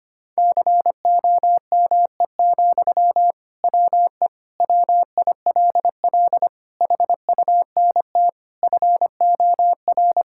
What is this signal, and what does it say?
Morse code, heard as a keyed tone, about 25 wpm, 700 Hz. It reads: COME, WE WILL HUNT FOR
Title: Sentences from Top 1600 Words 25wpm